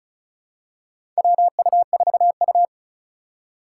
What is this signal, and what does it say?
Morse code, heard as a keyed tone, about 36 wpm, 700 Hz. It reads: WU4U
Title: USA Call Signs - 4 Characters - 4x Character Spacing - 35wpm